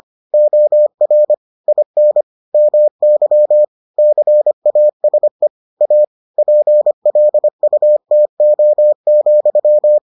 Morse code, heard as a keyed tone, about 25 wpm, 600 Hz. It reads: OR IN MY CASE A PLUTO,